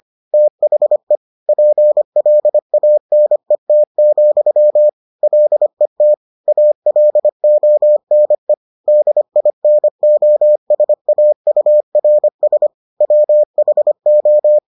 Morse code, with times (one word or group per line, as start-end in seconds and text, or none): 0.3-1.2 THE
1.5-4.9 PLANET,
5.2-6.1 LET
6.5-8.5 ALONE
8.9-12.7 DINOSAURS
13.0-14.6 WHO